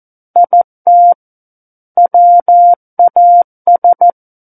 Morse code, heard as a keyed tone, about 14 wpm, 700 Hz. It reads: IT WAS
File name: Book One / 05-Chapter 5 - 14 WPM